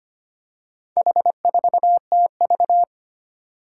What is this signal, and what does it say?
Morse code, heard as a keyed tone, about 25 wpm, 700 Hz. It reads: H4TV
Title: Four Challenging Letter-Number Combinations 25wpm